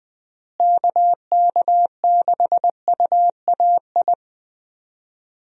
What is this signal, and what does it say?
Morse code, heard as a keyed tone, about 20 wpm, 700 Hz. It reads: KK6UAI
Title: USA Call Signs - 6 Characters - Repeated 3x - 20wpm